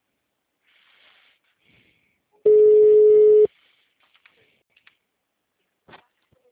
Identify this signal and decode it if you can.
Ringback tone (Europe, 425 Hz).